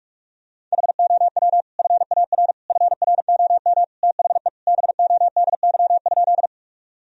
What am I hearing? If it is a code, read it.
Morse code, heard as a keyed tone, about 43 wpm, 700 Hz. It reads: HOW FAR FROM THE BODY?
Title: Sentences from Top 1000 Words - Review Part 2 - 45wpm